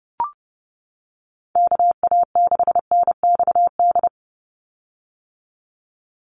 Morse code, heard as a keyed tone, about 30 wpm, 700 Hz. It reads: KA6NXD
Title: USA Call Signs 30wpm